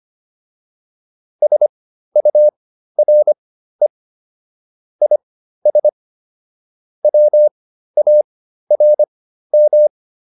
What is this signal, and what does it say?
Morse code, heard as a keyed tone, about 25 wpm, spaced as for 13 wpm, 600 Hz. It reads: SURE IS WARM